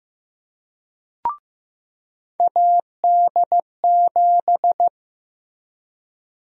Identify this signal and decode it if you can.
Morse code, heard as a keyed tone, about 15 wpm, 700 Hz. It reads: AD7
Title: USA Call Sign Prefixes - Repeated 2x - 15wpm